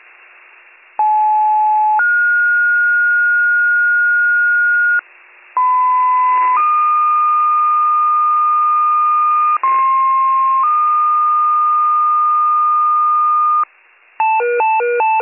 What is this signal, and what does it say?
Two-tone page, heard as a one-second tone, then a three-second tone, about 860 then 1430 Hz, about 1010 then 1220 Hz, about 1060 then 1220 Hz. It is heard from 1.0 to 13.6 s.